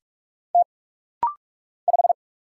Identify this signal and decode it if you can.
Morse code, heard as a keyed tone, about 46 wpm, 700 Hz. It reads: T 5
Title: Single Letter-Number - Rapid-Fire 45wpm